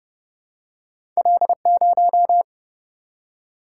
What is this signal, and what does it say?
Morse code, heard as a keyed tone, about 30 wpm, 700 Hz. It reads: L0
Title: Single-Letter Single-Number 30wpm